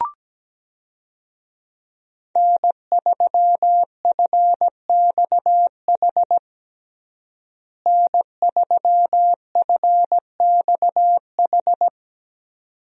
Morse code, heard as a keyed tone, about 17 wpm, 700 Hz. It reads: N3FXH N3FXH
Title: USA Call Signs - 5 Characters - Repeated 2x - 17wpm